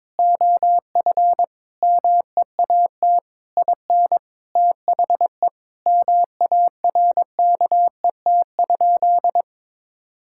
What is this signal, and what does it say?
Morse code, heard as a keyed tone, about 21 wpm, 700 Hz. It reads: OF MEAT IN THE MARKET?
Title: Sentences from Top 1000 Words - Review Part 6 - 22wpm